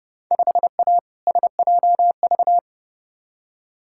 Morse code, heard as a keyed tone, about 31 wpm, 700 Hz. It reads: 5A SJV
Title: Field Day Exchanges - 30wpm